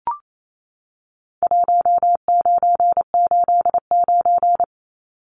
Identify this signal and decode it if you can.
Morse code, heard as a keyed tone, about 28 wpm, 700 Hz. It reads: 1989
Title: Numbers - 4-Digit 28wpm